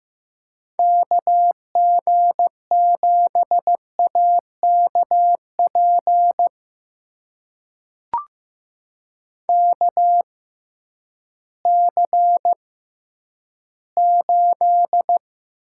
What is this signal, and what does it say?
Morse code, heard as a keyed tone, about 15 wpm, spaced as for 10 wpm, 700 Hz. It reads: KG7AKP KC8
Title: USA Call Signs - 6 Characters - 6x Character Spacing - 15wpm